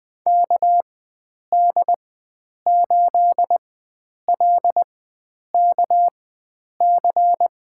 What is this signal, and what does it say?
Morse code, heard as a keyed tone, about 21 wpm, spaced as for 9 wpm, 700 Hz. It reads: KD8LKC
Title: USA Call Signs - 6 Characters - 4x Character Spacing - 20wpm